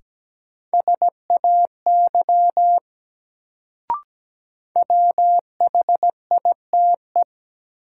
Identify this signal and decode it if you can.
Morse code, heard as a keyed tone, about 17 wpm, 700 Hz. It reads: SAY WHITE
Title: Sets of 2 Words - New Word Drill - 17wpm